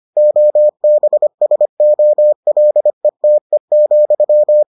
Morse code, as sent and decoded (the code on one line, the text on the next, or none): --- -... ... --- .-.. - . --..--
OBSOLETE,